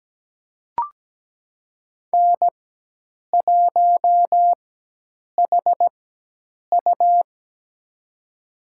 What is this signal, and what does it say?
Morse code, heard as a keyed tone, about 17 wpm, spaced as for 8 wpm, 700 Hz. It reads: N1HU
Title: USA Call Signs - 4x Character Spacing - 17wpm